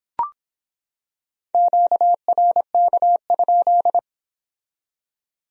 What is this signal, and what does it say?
Morse code, heard as a keyed tone, about 26 wpm, 700 Hz. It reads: QRK?